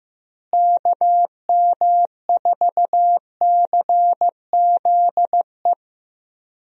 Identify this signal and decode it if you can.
Morse code, heard as a keyed tone, about 15 wpm, 700 Hz. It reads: KM4CZE
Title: USA Call Signs - 6 Characters - Repeated 3x - 15wpm